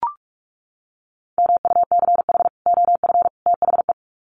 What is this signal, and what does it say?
Morse code, heard as a keyed tone, about 44 wpm, 700 Hz. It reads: MUCH OF THE
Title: Sets of 3 Words 45wpm